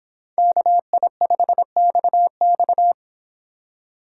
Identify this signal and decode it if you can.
Morse code, heard as a keyed tone, about 26 wpm, 700 Hz. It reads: KI5XX